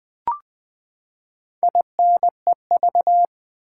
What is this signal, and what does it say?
Morse code, heard as a keyed tone, about 20 wpm, 700 Hz. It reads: INEV
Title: Build A Word - 4 Letters At A Time - 10-Letter Words - 20wpm